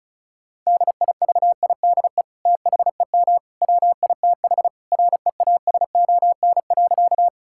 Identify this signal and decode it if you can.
Morse code, heard as a keyed tone, about 36 wpm, 700 Hz. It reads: DIVIDE THEM WITH REASON.
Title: Sentences from Top 1600 Words 35wpm